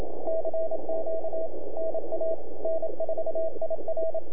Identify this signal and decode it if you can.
Morse code, heard as a keyed tone, about 27 wpm, 660 Hz. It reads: CQ NA N4IR
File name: NAQPCW-8-2009 / W1UJ NAQP CW 09 2230Z